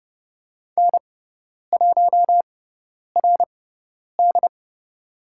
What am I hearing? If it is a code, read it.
Morse code, heard as a keyed tone, about 30 wpm, spaced as for 10 wpm, 700 Hz. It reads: N1RD
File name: Lesson 16g - Letter c - Call Signs 30wpm